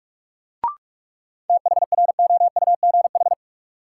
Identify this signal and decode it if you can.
Morse code, heard as a keyed tone, about 41 wpm, 700 Hz. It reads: THROUGH
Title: Sets of 2 Words - New Word Drill - 45wpm